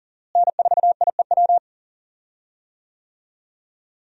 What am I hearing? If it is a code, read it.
Morse code, heard as a keyed tone, about 38 wpm, 700 Hz. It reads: N4IEW
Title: USA Call Signs - 5 Characters - Repeated 2x - 40wpm